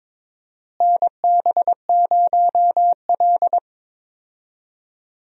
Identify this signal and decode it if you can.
Morse code, heard as a keyed tone, about 22 wpm, 700 Hz. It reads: NB0L
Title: USA Call Signs - 4 Characters - Repeated 2x - 22wpm